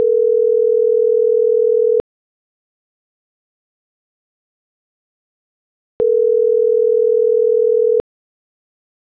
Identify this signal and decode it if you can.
Ringback tone (North America), 440 + 480 Hz.
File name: CENTRAL MORENA / out jacque.aguilera.25